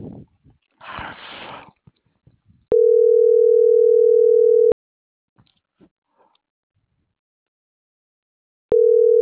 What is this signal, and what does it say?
Ringback tone (North America), 440 + 480 Hz.